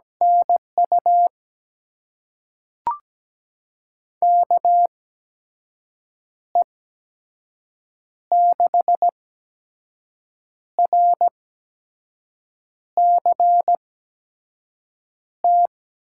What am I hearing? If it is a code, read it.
Morse code, heard as a keyed tone, about 17 wpm, spaced as for 5 wpm, 700 Hz. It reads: NU KE6RCT
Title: USA Call Signs - 6 Characters - 8x Character Spacing - 17wpm